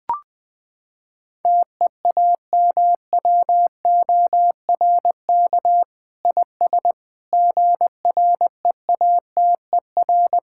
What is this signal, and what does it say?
Morse code, heard as a keyed tone, about 20 wpm, 700 Hz. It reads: TEAMWORK IS GREATER
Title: Sentences with 7-Word Phrases - 20wpm